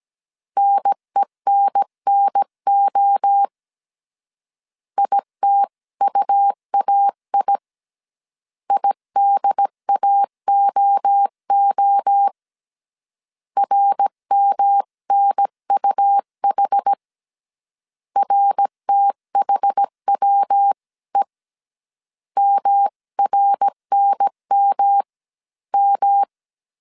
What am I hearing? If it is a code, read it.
Morse code, heard as a keyed tone, about 17 wpm, 800 Hz. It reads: NENNO ITUAI IDAOO RMNUH RTHWE MRNMM